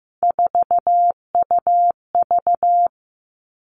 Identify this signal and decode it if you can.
Morse code, heard as a keyed tone, about 15 wpm, 700 Hz. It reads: 4UV
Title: Three Challenging Letter-Number Combinations 15wpm